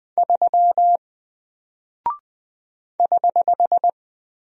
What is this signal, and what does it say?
Morse code, heard as a keyed tone, about 20 wpm, 700 Hz. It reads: 3 <HH>
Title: Single Character (Letter:Number:Punctuation:Prosign) Alt 20wpm